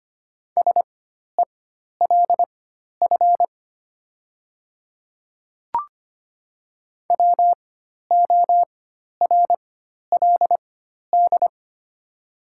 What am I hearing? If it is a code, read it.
Morse code, heard as a keyed tone, about 25 wpm, spaced as for 12 wpm, 700 Hz. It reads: SELF WORLD
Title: 4x Character Spacing - Top 200 Words - 25wpm